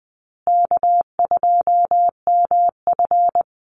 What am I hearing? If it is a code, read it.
Morse code, heard as a keyed tone, about 20 wpm, 700 Hz. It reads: K2MF
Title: Easy Call Signs Repeated 2x - 20wpm